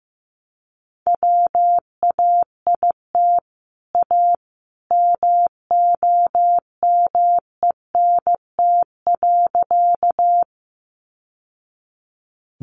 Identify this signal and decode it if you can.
Morse code, heard as a keyed tone, about 15 wpm, 700 Hz. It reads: WAIT A MOMENT.
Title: Sentences from Top 1000 Words 15wpm